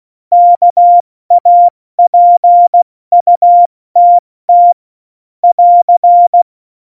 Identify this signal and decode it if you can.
Morse code, heard as a keyed tone, about 16 wpm, 700 Hz. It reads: KAPUTT <AR>
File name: Morsekurs / Witze / Witz 29. Januar 2019, 70 F80